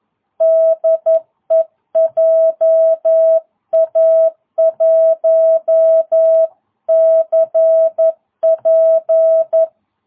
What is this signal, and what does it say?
Morse code, heard as a keyed tone, about 11 wpm, 660 Hz. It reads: DE JA1CP
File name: jh8cbh / 2022cw